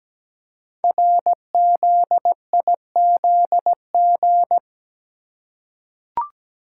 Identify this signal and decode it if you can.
Morse code, heard as a keyed tone, about 17 wpm, 700 Hz. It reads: RZIZG
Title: Code Groups - 5 Letters - 17wpm